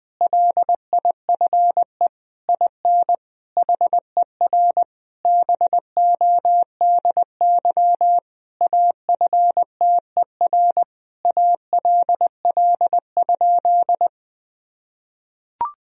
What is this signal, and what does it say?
Morse code, heard as a keyed tone, about 20 wpm, 700 Hz. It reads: LIFE IN HER BODY AFTER ALL?
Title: Sentences from Top 1000 Words - Review Part 2 - 20wpm